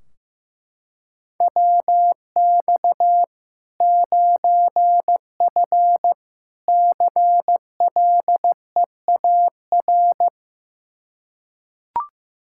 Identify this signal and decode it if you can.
Morse code, heard as a keyed tone, about 15 wpm, 700 Hz. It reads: WX 9F CLEAR
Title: K1USN SST QSO Practice 15wpm